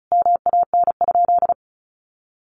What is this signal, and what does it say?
Morse code, heard as a keyed tone, about 34 wpm, 700 Hz. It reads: MAN?
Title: Sentences from Top 1300 Words 35wpm